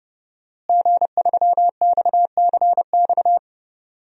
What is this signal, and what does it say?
Morse code, heard as a keyed tone, about 30 wpm, 700 Hz. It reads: G3XCX
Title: DXCC-100 Call Signs - 5 Characters - Repeated 3x - 30wpm